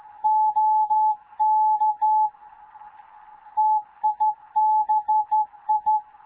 Morse code, heard as a keyed tone, about 15 wpm, 860 Hz. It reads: OK TIBI
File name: Morze-HA2MN / hangp / 2 11 GYAKORLAT